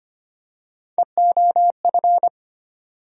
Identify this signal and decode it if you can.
Morse code, heard as a keyed tone, about 25 wpm, 700 Hz. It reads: EOF
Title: USA Call Sign Postfixes - 3 Letters 25wpm